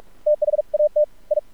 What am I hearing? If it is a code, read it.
Morse code, heard as a keyed tone, about 46 wpm, 600 Hz. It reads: THAT I
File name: archived / CRSnet